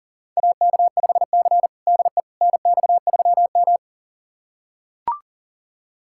Morse code, heard as a keyed tone, about 37 wpm, 700 Hz. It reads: AK5C DE NX3M
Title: USA Callsign DE Callsign - 4 Characters - 40wpm